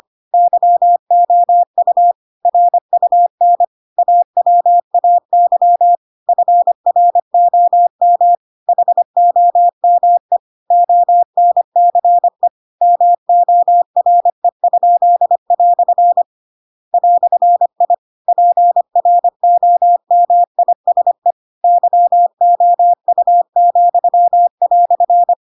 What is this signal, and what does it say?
Morse code, heard as a keyed tone, about 25 wpm, 700 Hz. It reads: YOU RUN AWAY FROM HOME ONCE MORE?" "I PROMISE YOU,"